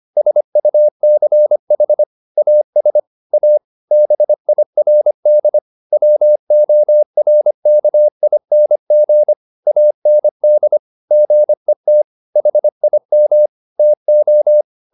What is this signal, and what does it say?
Morse code, heard as a keyed tone, about 25 wpm, 600 Hz. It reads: SUCH AS A BIRD WORKING AND GET HIM TO
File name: the old man and the sea part 1 25wpm